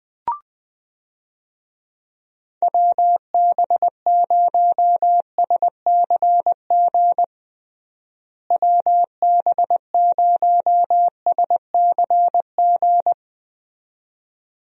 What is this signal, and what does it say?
Morse code, heard as a keyed tone, about 20 wpm, 700 Hz. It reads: WB0SCG WB0SCG
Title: USA Call Signs - 6 Characters - Repeated 2x - 20wpm